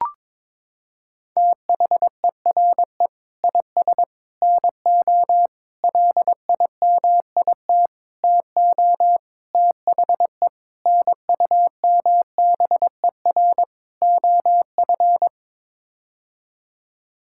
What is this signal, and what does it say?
Morse code, heard as a keyed tone, about 22 wpm, 700 Hz. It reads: THERE IS NO LIMIT TO THE NUMBER OF
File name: practice / 22wpm / Sets of 8 Words - 22wpm